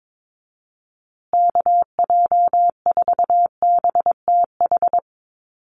Morse code, heard as a keyed tone, about 22 wpm, 700 Hz. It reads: KJ4BTH
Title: USA Call Signs Repeated 3x 22wpm